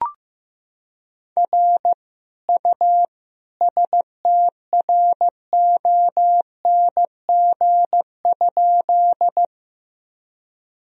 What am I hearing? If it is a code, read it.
Morse code, heard as a keyed tone, about 15 wpm, 700 Hz. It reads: R U STRONG?
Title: Sentences from Top 500 Words and Abbreviations 15wpm